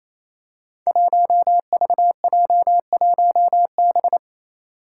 Morse code, heard as a keyed tone, about 28 wpm, 700 Hz. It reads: 1VJ1B
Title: Five Challenging Letter-Number Combinations 28wpm